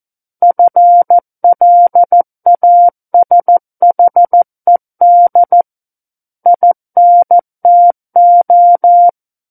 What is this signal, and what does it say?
Morse code, heard as a keyed tone, about 14 wpm, 700 Hz. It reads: FLASHED INTO
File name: Book One / 05-Chapter 5 - 14 WPM